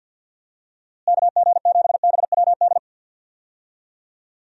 Morse code, heard as a keyed tone, about 52 wpm, 700 Hz. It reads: KG7BWD
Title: USA Call Signs - 6 Characters - Repeated 3x - 50wpm